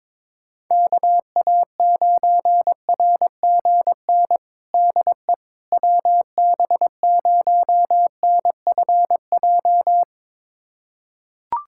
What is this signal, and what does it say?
Morse code, heard as a keyed tone, about 22 wpm, 700 Hz. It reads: KA9RGN DE WB0NFJ
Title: USA Callsign DE Callsign - 6 Characters - 22wpm